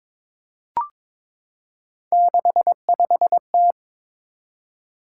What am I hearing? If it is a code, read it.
Morse code, heard as a keyed tone, about 23 wpm, 700 Hz. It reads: 65T
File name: practice / 22wpm / Three Letter-Number Combinations - 22wpm